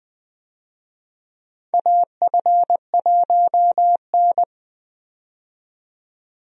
Morse code, heard as a keyed tone, about 21 wpm, 700 Hz. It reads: AF1N
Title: USA Call Signs - 4 Characters - 20wpm